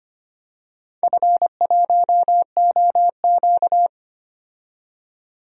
Morse code, heard as a keyed tone, about 25 wpm, 700 Hz. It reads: F1OQ
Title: DXCC-100 Call Signs - 4 Characters - 25wpm